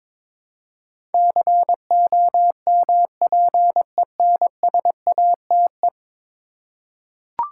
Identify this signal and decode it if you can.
Morse code, heard as a keyed tone, about 21 wpm, 700 Hz. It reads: COMPENSATE
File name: practice / 22wpm / 10 Letter Words - First 3 Letters Given - 22wpm